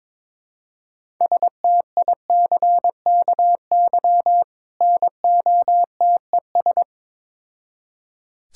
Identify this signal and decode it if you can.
Morse code, heard as a keyed tone, about 22 wpm, 700 Hz. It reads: STICKY NOTES
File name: Difficult Words with Hints - 22wpm